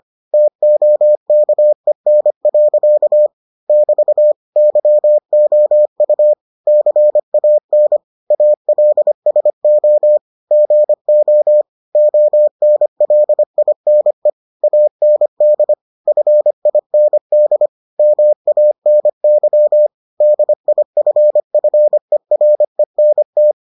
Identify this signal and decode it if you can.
Morse code, heard as a keyed tone, about 25 wpm, 600 Hz. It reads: TOKEN. <BT> YOU CAN ALSO GO ONLINE AND FIND MANY DIFFERENT